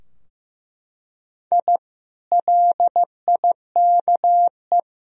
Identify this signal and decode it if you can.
Morse code, heard as a keyed tone, about 15 wpm, 700 Hz. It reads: I LIKE
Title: Sentences from Top 1000 Words - Review Part 3 - 15wpm